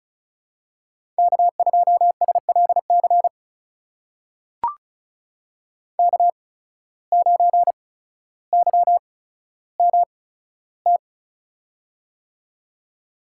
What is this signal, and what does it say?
Morse code, heard as a keyed tone, about 36 wpm, spaced as for 15 wpm, 700 Hz. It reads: K2SLC K9YMT
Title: USA Call Signs - 5 Characters - 8x Character Spacing - 35wpm